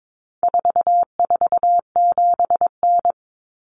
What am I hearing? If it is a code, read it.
Morse code, heard as a keyed tone, about 22 wpm, 700 Hz. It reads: 447N
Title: Common Cut Numbers - 4-Digit 22wpm